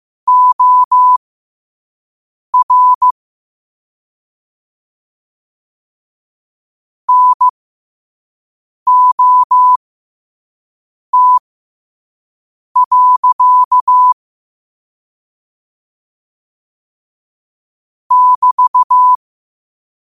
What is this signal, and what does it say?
Morse code, heard as a keyed tone, about 15 wpm, spaced as for 5 wpm, 1000 Hz. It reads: OR NOT. <BT>
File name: qotdfiles / 5wpm